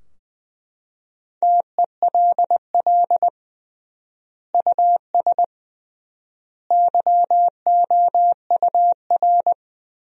Morse code, heard as a keyed tone, about 21 wpm, 700 Hz. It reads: TELL US YOUR
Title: 3x Word-Spacing - Sentences from Top 200 Words - 20wpm